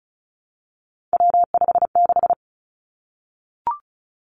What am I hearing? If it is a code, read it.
Morse code, heard as a keyed tone, about 36 wpm, 700 Hz. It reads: W56
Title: Three Challenging Letter-Number Combinations 35wpm